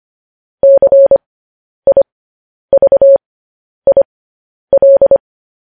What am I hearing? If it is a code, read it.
Morse code, heard as a keyed tone, about 25 wpm, spaced as for 10 wpm, 550 Hz. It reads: CIVIL
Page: news-headlines-in-morse-code-at-10-wpm